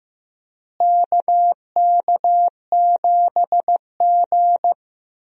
Morse code, heard as a keyed tone, about 15 wpm, 700 Hz. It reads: KK7G
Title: USA Call Signs - 4 Characters - Repeated 3x - 15wpm